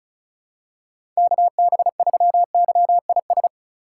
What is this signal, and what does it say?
Morse code, heard as a keyed tone, about 34 wpm, 700 Hz. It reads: KB3YIS
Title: USA Call Signs - 6 Characters - 2x Character Spacing - 35wpm